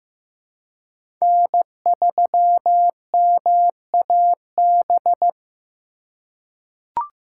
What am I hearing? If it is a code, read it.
Morse code, heard as a keyed tone, about 16 wpm, 700 Hz. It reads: N3MAB
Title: USA Call Signs - 5 Characters - 3x Character Spacing - 15wpm